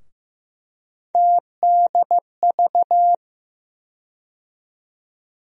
Morse code, heard as a keyed tone, about 15 wpm, 700 Hz. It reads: TDV
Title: USA Call Sign Postfixes - 3 Letters 15wpm